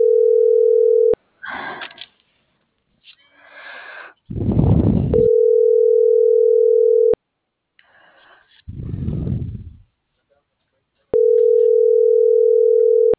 Ringback tone (North America), 440 + 480 Hz.
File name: CENTRAL MULTIMODAL / out viridiana.juarez - .81396